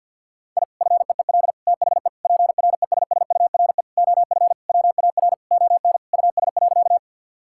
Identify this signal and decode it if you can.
Morse code, heard as a keyed tone, about 50 wpm, 700 Hz. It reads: I FEEL THE PRESSURE OF WAR ON US.
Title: Sentences from Top 1000 Words - Review Part 6 - 50wpm